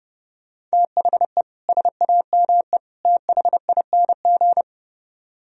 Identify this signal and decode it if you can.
Morse code, heard as a keyed tone, about 33 wpm, 700 Hz. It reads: THE SAME THING